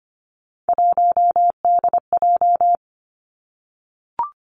Morse code, heard as a keyed tone, about 25 wpm, 700 Hz. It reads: 1DJ
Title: Three Challenging Letter-Number Combinations 25wpm